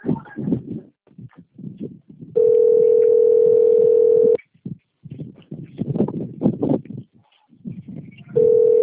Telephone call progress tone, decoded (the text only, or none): ringback tone